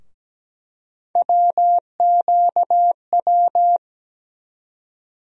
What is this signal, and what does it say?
Morse code, heard as a keyed tone, about 17 wpm, 700 Hz. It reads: WQW
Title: USA Call Sign Postfixes - 3 Letters 17wpm